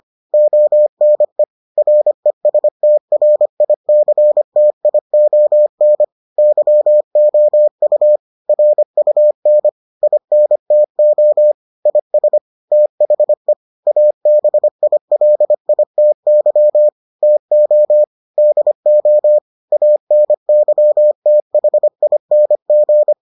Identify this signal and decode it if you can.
Morse code, heard as a keyed tone, about 25 wpm, 600 Hz. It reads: ONE RESTRICTION YOU RUN INTO IS THE ABILITY TO DO ANYTHING